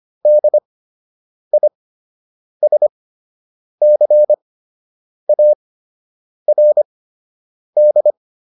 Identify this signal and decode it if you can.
Morse code, heard as a keyed tone, about 25 wpm, spaced as for 8 wpm, 600 Hz. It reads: DISCARD